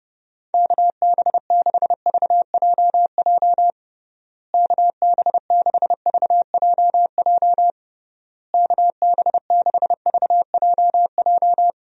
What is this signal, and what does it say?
Morse code, heard as a keyed tone, about 30 wpm, 700 Hz. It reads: KB6VJJ KB6VJJ KB6VJJ
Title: USA Call Signs - 6 Characters - Repeated 3x - 30wpm